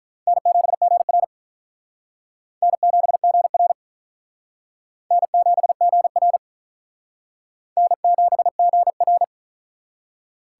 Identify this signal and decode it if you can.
Morse code, heard as a keyed tone, about 48 wpm, 700 Hz. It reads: N7GR N7GR N7GR N7GR